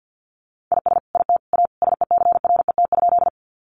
Morse code, heard as a keyed tone, about 50 wpm, 700 Hz. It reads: IS IT A SECRET?
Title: Sentences from Top 1200 Words 50wpm